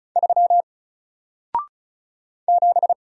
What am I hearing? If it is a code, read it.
Morse code, heard as a keyed tone, about 35 wpm, 700 Hz. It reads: 3 7
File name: Single-Letter-Number-35wpm